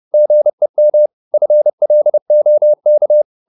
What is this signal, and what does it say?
Morse code, heard as a keyed tone, about 30 wpm, 600 Hz. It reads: GEM FLOK
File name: Korte ord DK 30wpm